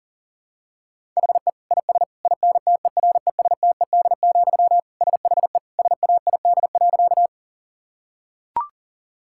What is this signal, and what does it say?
Morse code, heard as a keyed tone, about 40 wpm, 700 Hz. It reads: HE IS INTERESTED, SHE SAID.
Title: Sentences from Top 1000 Words - Review Part 6 - 40wpm